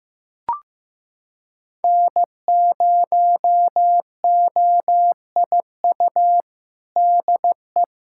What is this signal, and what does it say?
Morse code, heard as a keyed tone, about 15 wpm, 700 Hz. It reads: N0OIU DE
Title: USA Callsign DE Callsign 15wpm